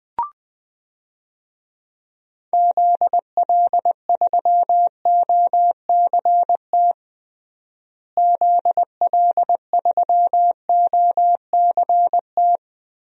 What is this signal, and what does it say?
Morse code, heard as a keyed tone, about 20 wpm, 700 Hz. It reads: ZL3OCT ZL3OCT
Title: DXCC-100 Call Signs - 6 Characters - Repeated 3x - 20wpm